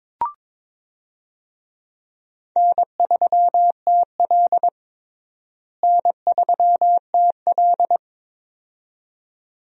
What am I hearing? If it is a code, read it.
Morse code, heard as a keyed tone, about 22 wpm, 700 Hz. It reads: N3TL N3TL
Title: USA Call Signs - 4 Characters - Repeated 2x - 22wpm